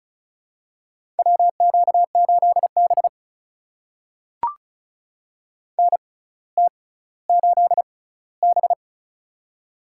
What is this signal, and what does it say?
Morse code, heard as a keyed tone, about 36 wpm, spaced as for 18 wpm, 700 Hz. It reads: WQ8B NT8B
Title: USA Call Signs - 4 Characters - 6x Character Spacing - 35wpm